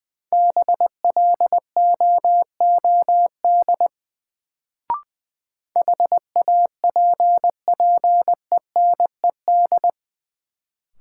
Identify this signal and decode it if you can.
Morse code, heard as a keyed tone, about 20 wpm, 700 Hz. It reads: BLOOD HAPPENED
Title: Sentences from Top 1000 Words - Review Part 2 - New Word Drill - 20wpm